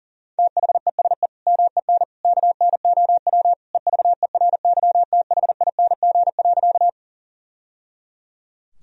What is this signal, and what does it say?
Morse code, heard as a keyed tone, about 40 wpm, 700 Hz. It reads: THESE MEN KNOW EVERYTHING.
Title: Sentences from Top 1000 Words - Review Part 1 - 40wpm